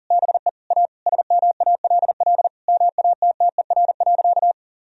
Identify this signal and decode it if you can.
Morse code, heard as a keyed tone, about 40 wpm, 700 Hz. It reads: BE A SMALL MATTER.
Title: Sentences from Top 1000 Words - Review Part 1 - 40wpm